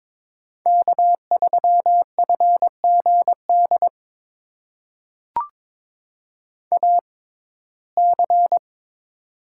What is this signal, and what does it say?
Morse code, heard as a keyed tone, about 22 wpm, spaced as for 16 wpm, 700 Hz. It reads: K3FGD AC
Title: USA Call Signs - 5 Characters - 6x Character Spacing - 22wpm